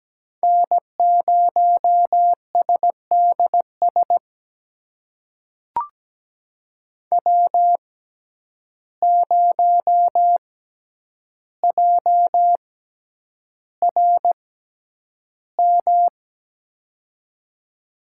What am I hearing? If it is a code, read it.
Morse code, heard as a keyed tone, about 17 wpm, spaced as for 9 wpm, 700 Hz. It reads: N0SDS W0JRM
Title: USA Call Signs - 5 Characters - 6x Character Spacing - 17wpm